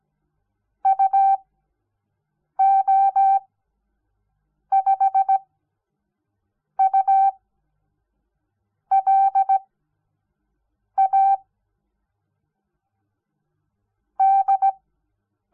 Morse code, heard as a keyed tone, about 17 wpm, spaced as for 6 wpm, 780 Hz. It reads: UO5U LA D
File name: Onlinelehrgang / morsen / DJ4UF Morsen06